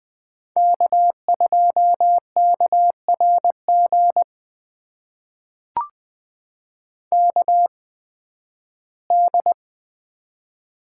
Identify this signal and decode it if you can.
Morse code, heard as a keyed tone, about 20 wpm, spaced as for 13 wpm, 700 Hz. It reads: K2KRG KD
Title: USA Call Signs - 8x Character Spacing - 20wpm